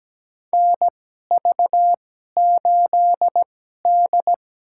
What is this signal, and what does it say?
Morse code, heard as a keyed tone, about 17 wpm, spaced as for 12 wpm, 700 Hz. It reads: NV8D
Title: USA Call Signs - 4 Characters - 2x Character Spacing - 17wpm